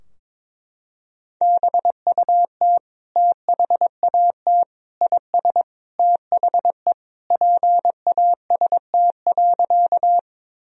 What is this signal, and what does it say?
Morse code, heard as a keyed tone, about 22 wpm, 700 Hz. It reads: BUT THAT IS THE PAST.
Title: Sentences from Top 1000 Words - Review Part 3 - 22wpm